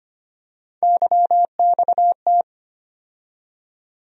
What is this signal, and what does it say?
Morse code, heard as a keyed tone, about 25 wpm, 700 Hz. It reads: YXT
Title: USA Call Sign Postfixes - 3 Letters 25wpm